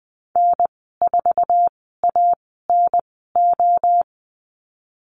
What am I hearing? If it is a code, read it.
Morse code, heard as a keyed tone, about 20 wpm, 700 Hz. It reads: N 4 A N O
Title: 048 Lesson 9g - Number 4 - Call Signs 20wpm